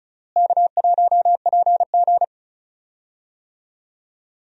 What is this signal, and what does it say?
Morse code, heard as a keyed tone, about 36 wpm, 700 Hz. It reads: K1PG
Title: USA Call Signs - 4 Characters - Repeated 3x - 35wpm